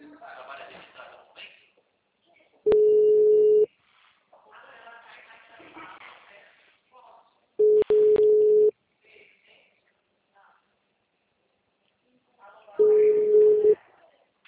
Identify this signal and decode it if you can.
Ringback tone (Europe, 425 Hz), with speech.